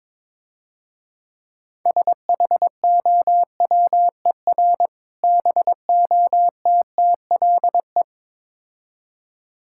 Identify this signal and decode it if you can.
Morse code, heard as a keyed tone, about 22 wpm, 700 Hz. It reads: SHOWER BOTTLE
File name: Easy Words - Speed-Racing with Hints - 22wpm